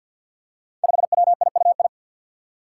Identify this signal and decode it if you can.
Morse code, heard as a keyed tone, about 53 wpm, 700 Hz. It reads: 5WIUI
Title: Five Challenging Letter-Number Combinations 50wpm